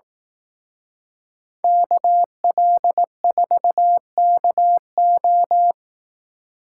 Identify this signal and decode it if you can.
Morse code, heard as a keyed tone, about 18 wpm, 700 Hz. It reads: KL4KO
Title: Speed Racing - USA 2x2 Call Signs 15wpm